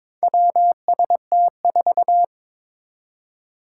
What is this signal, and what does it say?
Morse code, heard as a keyed tone, about 22 wpm, 700 Hz. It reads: WST4
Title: Four Challenging Letter-Number Combinations 22wpm